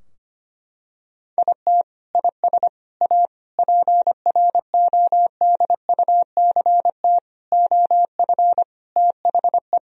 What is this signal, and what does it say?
Morse code, heard as a keyed tone, about 25 wpm, 700 Hz. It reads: IT IS A PRODUCT OF THE
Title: Sentences from Top 1000 Words - Review Part 4 - 25wpm